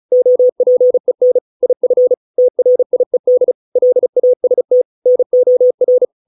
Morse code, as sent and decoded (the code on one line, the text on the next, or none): --- .--. -. / .. ..-. / - .-. .. -.. / .-.. .- ... - / -. --- .-.
OPEN IF TRIED LAST NOR